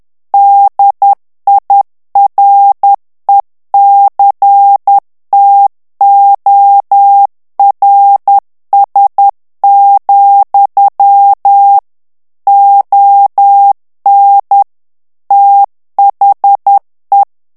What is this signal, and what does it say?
Morse code, heard as a keyed tone, about 10 wpm, 800 Hz. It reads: DIRECTORS, ON THE
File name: ANGLAIS texte clair 3 vitesse 10 mots minute